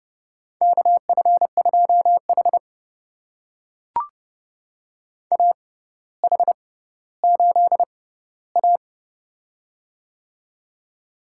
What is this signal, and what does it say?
Morse code, heard as a keyed tone, about 30 wpm, spaced as for 15 wpm, 700 Hz. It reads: KF2H AH8A
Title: USA Call Signs - 4 Characters - 6x Character Spacing - 30wpm